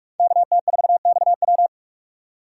Morse code, heard as a keyed tone, about 44 wpm, 700 Hz. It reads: KT4XW